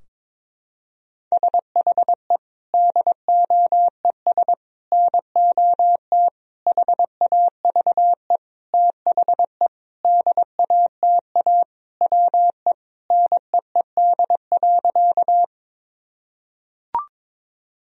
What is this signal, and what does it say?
Morse code, heard as a keyed tone, about 22 wpm, 700 Hz. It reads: SHE DOES NOT HAVE THE DATA WE NEED.
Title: Sentences from Top 1000 Words - Review Part 3 - 22wpm